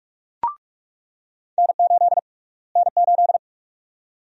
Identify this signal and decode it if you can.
Morse code, heard as a keyed tone, about 42 wpm, 700 Hz. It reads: N8 N8
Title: USA Call Sign Prefixes - Repeated 2x - 45wpm